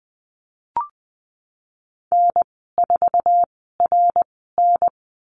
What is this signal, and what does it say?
Morse code, heard as a keyed tone, about 20 wpm, spaced as for 15 wpm, 700 Hz. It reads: N4RN
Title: 055 Lesson 10g - Letter r - Call Signs 20wpm